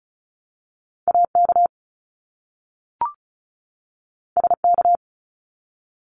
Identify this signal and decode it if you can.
Morse code, heard as a keyed tone, about 34 wpm, 700 Hz. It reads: AK SK
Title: ARRL Sections 35wpm